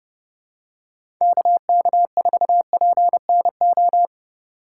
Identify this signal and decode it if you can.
Morse code, heard as a keyed tone, about 30 wpm, 700 Hz. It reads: KK4PNO
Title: USA Call Signs - 6 Characters - 4x Character Spacing - 30wpm